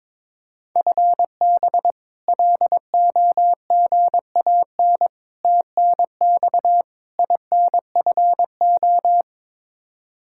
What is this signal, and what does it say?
Morse code, heard as a keyed tone, about 22 wpm, 700 Hz. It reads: FB LOGAN TNX INFO
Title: K1USN SST QSO Practice 22wpm